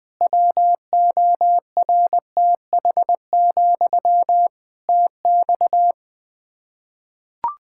Morse code, heard as a keyed tone, about 20 wpm, 700 Hz. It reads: WORTH, TX
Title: Practice QSO - Exchange 1 - Beginning - 20wpm